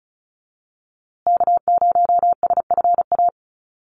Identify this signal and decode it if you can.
Morse code, heard as a keyed tone, about 34 wpm, 700 Hz. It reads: K0SFA
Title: USA Call Signs Repeated 2x 35wpm